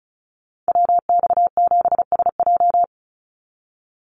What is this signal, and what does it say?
Morse code, heard as a keyed tone, about 34 wpm, 700 Hz. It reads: WX7SJ
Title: Call Signs Repeated 2x - 35wpm